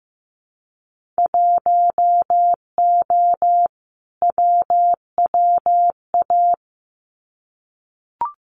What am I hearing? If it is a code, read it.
Morse code, heard as a keyed tone, about 15 wpm, 700 Hz. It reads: 1O WWA
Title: Winter Field Day Exchanges 15wpm